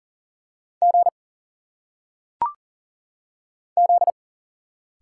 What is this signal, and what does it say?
Morse code, heard as a keyed tone, about 40 wpm, 700 Hz. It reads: G Z
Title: Single Letters - Phonetic Pronunciation 40wpm